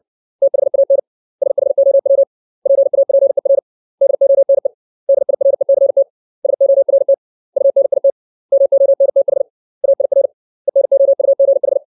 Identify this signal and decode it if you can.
Morse code, heard as a keyed tone, about 62 wpm, 550 Hz. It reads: THAN SHOW PAPER DONE DIRECT SORT UNIT MONTH AIR ENOUGH
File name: 632 words / 632 Words 60WPM